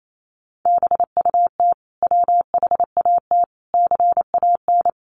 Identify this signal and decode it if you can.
Morse code, heard as a keyed tone, about 28 wpm, 700 Hz. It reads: BUT WHAT CAN
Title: Sentences from Top 1000 Words 28wpm